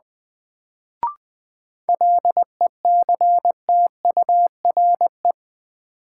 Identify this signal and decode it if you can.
Morse code, heard as a keyed tone, about 20 wpm, 700 Hz. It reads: LECTURE
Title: Sentences from Top 1600 Words - New Word Drill - 20wpm